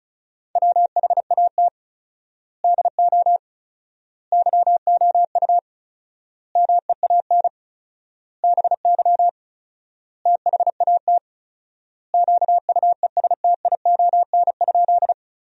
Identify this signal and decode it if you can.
Morse code, heard as a keyed tone, about 34 wpm, 700 Hz. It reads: WHAT DO YOU MEAN BY THAT QUESTION?
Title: 4x Word-Spacing - Sentences from Top 400 Words - 35wpm